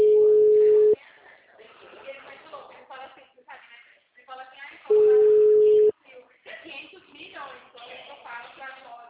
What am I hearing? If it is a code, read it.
Ringback tone (Europe, 425 Hz), with speech.